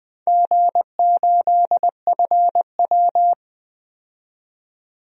Morse code, heard as a keyed tone, about 20 wpm, 700 Hz. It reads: G8FW
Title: DXCC-100 Call Signs - 4 Characters - 20wpm